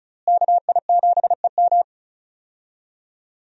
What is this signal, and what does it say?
Morse code, heard as a keyed tone, about 36 wpm, 700 Hz. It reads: KI7EM
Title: USA Call Signs - 5 Characters - Repeated 2x - 35wpm